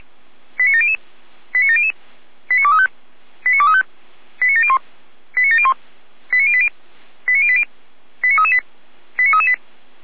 Five-tone selective call (ZVEI), heard as quick runs of five tones, tones of about 70 ms. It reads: 89700 89700 89235 89235 89701 89701 89080 89080 89338 89338